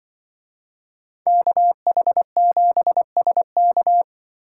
Morse code, heard as a keyed tone, about 24 wpm, 700 Hz. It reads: KH7SK
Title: Speed Racing - USA Call Signs 20wpm